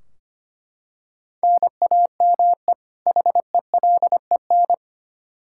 Morse code, heard as a keyed tone, about 25 wpm, 700 Hz. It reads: NAME HELEN
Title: Practice QSO - Exchange 1 - Beginning - 25wpm